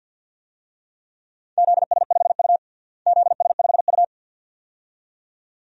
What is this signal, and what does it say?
Morse code, heard as a keyed tone, about 47 wpm, 700 Hz. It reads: ZS5U ZS5U
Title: DXCC-100 Call Signs - 4 Characters - Repeated 2x - 50wpm